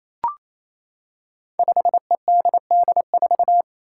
Morse code, heard as a keyed tone, about 28 wpm, 700 Hz. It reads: 5EDD4
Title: Five Challenging Letter-Number Combinations 28wpm